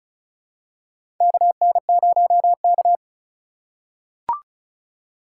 Morse code, heard as a keyed tone, about 35 wpm, 690 Hz. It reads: KN0K
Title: USA Call Signs - 4 Characters - 2x Character Spacing - 35wpm